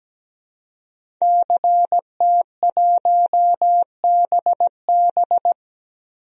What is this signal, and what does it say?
Morse code, heard as a keyed tone, about 17 wpm, 700 Hz. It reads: CT1BB
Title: DXCC-100 Call Signs - 5 Characters - Repeated 2x - 17wpm